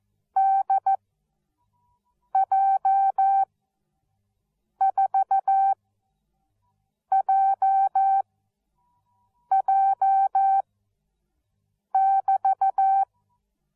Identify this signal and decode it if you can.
Morse code, heard as a keyed tone, about 14 wpm, spaced as for 5 wpm, 780 Hz. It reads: DJ4JJ<BT>